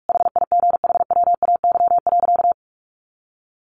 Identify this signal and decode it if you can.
Morse code, heard as a keyed tone, about 44 wpm, 700 Hz. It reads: HIGHWAY.
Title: Sentences from Top 1400 Words 45wpm